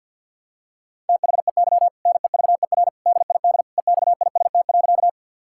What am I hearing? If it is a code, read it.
Morse code, heard as a keyed tone, about 52 wpm, 700 Hz. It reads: THEY NEVER DID EXIST.
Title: Sentences from Top 1000 Words - Review Part 5 - 50wpm